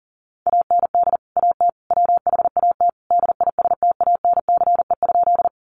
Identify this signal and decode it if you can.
Morse code, heard as a keyed tone, about 40 wpm, 700 Hz. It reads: AND AT WHAT DISTANCE?
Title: Sentences from Top 1200 Words 40wpm